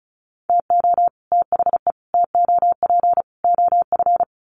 Morse code, heard as a keyed tone, about 34 wpm, 700 Hz. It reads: TO THE TOP OF
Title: Sets of 4 Words 35wpm